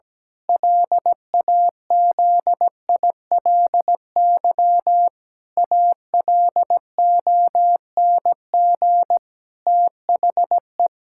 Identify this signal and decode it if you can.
Morse code, heard as a keyed tone, about 17 wpm, 700 Hz. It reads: LAZILY ALONG THE
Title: Pangram Challenge v2 - 17wpm